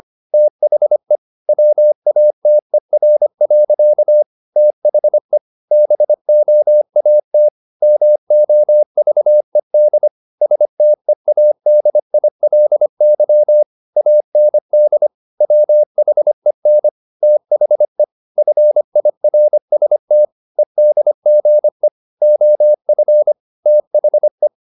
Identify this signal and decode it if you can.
Morse code, heard as a keyed tone, about 25 wpm, 600 Hz. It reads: THE WATER. THE BOAT MOVED STEADILY AND WHEN THE FIRST EDGE OF THE
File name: the old man and the sea part 1 25wpm